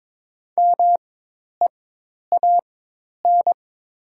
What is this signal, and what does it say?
Morse code, heard as a keyed tone, about 22 wpm, spaced as for 10 wpm, 700 Hz. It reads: MEAN
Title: 4x Character Spacing - Top 300 Words - 22wpm